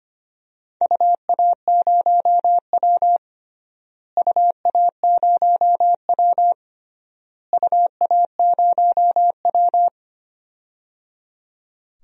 Morse code, heard as a keyed tone, about 25 wpm, 700 Hz. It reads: UA0W UA0W UA0W
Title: DXCC-100 Call Signs - 4 Characters - Repeated 3x - 25wpm